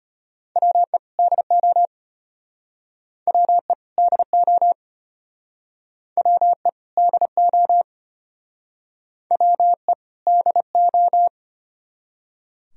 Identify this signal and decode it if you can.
Morse code, heard as a keyed tone, about 32 wpm, 700 Hz. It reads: WE DO WE DO WE DO WE DO